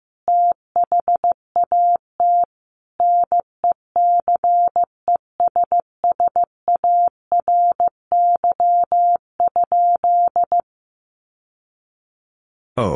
Morse code, heard as a keyed tone, about 14 wpm, 700 Hz. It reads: THAT NECESSARY?
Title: Sentences from Top 800 Words 15wpm